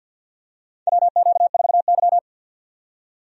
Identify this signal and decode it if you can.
Morse code, heard as a keyed tone, about 47 wpm, 700 Hz. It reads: WQ4Y
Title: USA Call Signs - 4 Characters - 50wpm